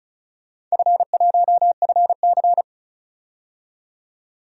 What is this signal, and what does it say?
Morse code, heard as a keyed tone, about 36 wpm, 700 Hz. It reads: F1FC
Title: DXCC-100 Call Signs - 4 Characters - 35wpm